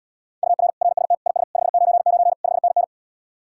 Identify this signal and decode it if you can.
Morse code, heard as a keyed tone, about 79 wpm, 700 Hz. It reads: UR RST IS 599 5NN